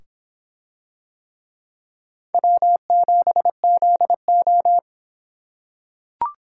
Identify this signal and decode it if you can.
Morse code, heard as a keyed tone, about 26 wpm, 700 Hz. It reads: W7ZO